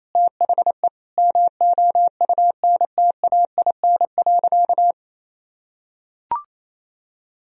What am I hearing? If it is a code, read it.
Morse code, heard as a keyed tone, about 28 wpm, 700 Hz. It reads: THE MOUNTAIN.
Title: Sentences from Top 1000 Words - Review Part 1 - 28wpm